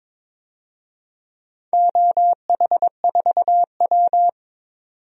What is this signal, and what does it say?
Morse code, heard as a keyed tone, about 22 wpm, 700 Hz. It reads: OH4W
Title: DXCC-100 Call Signs - 4 Characters - 22wpm